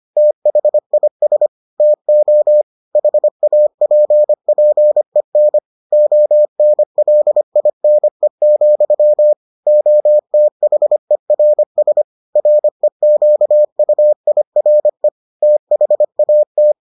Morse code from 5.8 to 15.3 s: ONLINE, OTHERS REQUIRE

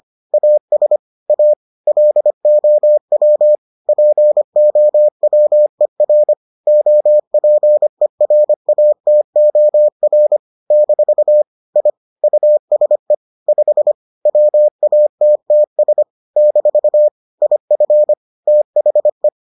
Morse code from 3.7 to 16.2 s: POWER OPERATOR - I USE 5 WATTS

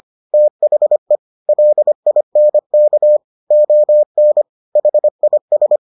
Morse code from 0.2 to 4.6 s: THE LINK ON

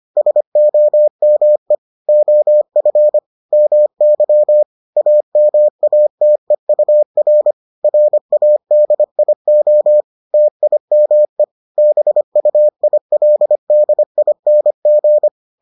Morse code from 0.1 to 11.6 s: SOME OF MY AMATEUR RADIO TIME

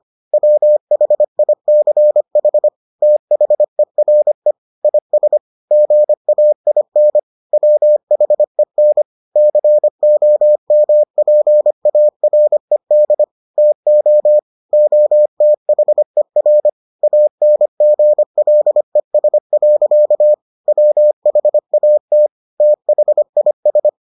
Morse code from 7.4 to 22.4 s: WHEN COMPARED TO OTHER ANGLES. WHAT

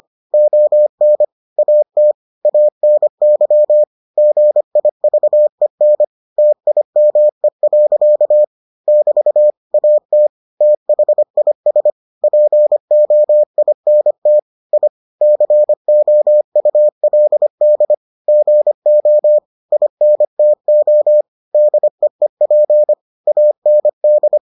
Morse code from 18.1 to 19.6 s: GO

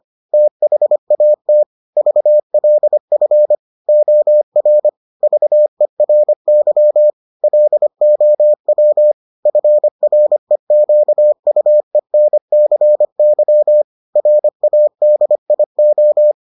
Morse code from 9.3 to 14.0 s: FREQUENCY

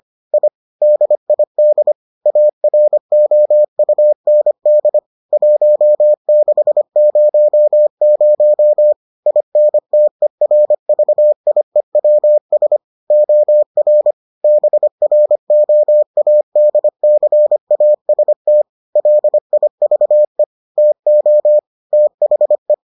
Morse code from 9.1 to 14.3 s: INTERVIEWS OR

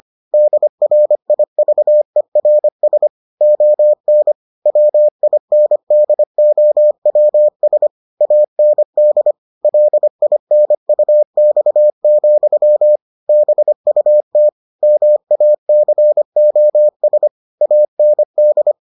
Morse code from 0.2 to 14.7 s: DRIVERS ON WINDOWS AND LINUX, BUT

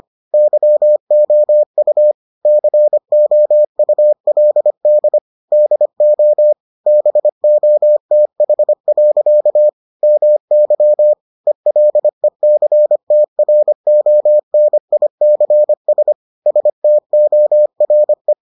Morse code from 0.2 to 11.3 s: YOU COULD DO BOTH. MY

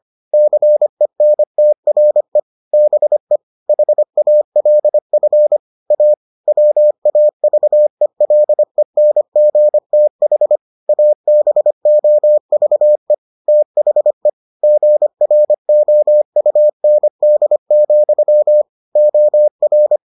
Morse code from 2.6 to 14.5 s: BE HALF A WAVELENGTH ABOVE THE

